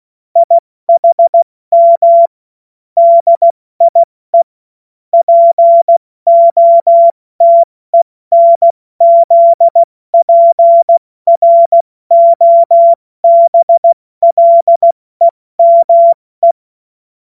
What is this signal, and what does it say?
Morse code, heard as a keyed tone, about 16 wpm, 680 Hz. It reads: IHM DIE POTENZPROBLEME